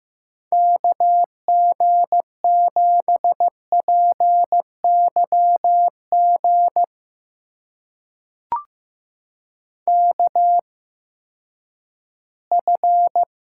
Morse code, heard as a keyed tone, about 15 wpm, spaced as for 10 wpm, 700 Hz. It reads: KG7PYG KF
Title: USA Call Signs - 6 Characters - 8x Character Spacing - 15wpm